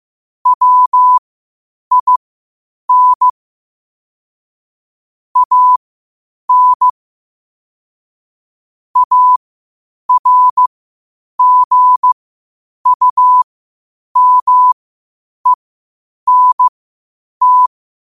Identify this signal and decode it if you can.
Morse code, heard as a keyed tone, about 15 wpm, spaced as for 8 wpm, 1000 Hz. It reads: WIN AN ARGUMENT
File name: qotdfiles / 8wpm